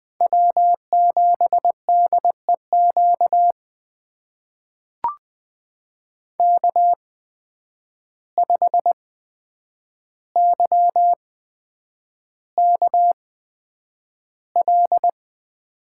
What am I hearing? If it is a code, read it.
Morse code, heard as a keyed tone, about 20 wpm, spaced as for 9 wpm, 700 Hz. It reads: W7DEQ K5YKL
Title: USA Call Signs - 5 Characters - 8x Character Spacing - 20wpm